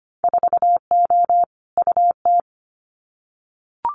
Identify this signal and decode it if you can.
Morse code, heard as a keyed tone, about 25 wpm, 690 Hz. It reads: 4O UT
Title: Winter Field Day Exchanges Encore 25wpm